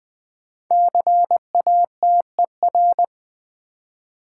Morse code, heard as a keyed tone, about 20 wpm, 700 Hz. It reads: CATER
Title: 5 Letter Words - First 3 Letters Given - 20wpm